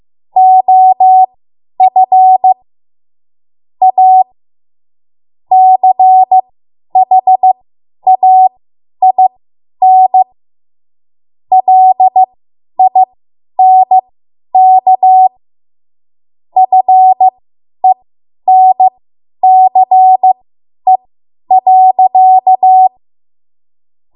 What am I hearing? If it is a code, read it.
Morse code, heard as a keyed tone, about 15 wpm, spaced as for 10 wpm, 750 Hz. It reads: OF A CHAIN LINK FENCE.
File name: Morse / 10WPM